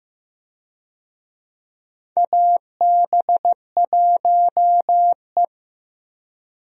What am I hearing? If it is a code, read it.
Morse code, heard as a keyed tone, about 15 wpm, 700 Hz. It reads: AB1E
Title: USA Call Signs - 4 Characters - Repeated 3x - 15wpm